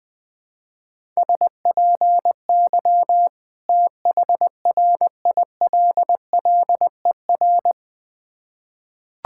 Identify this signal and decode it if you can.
Morse code, heard as a keyed tone, about 21 wpm, 700 Hz. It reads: SPY THRILLER
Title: Difficult Words with Hints - 20wpm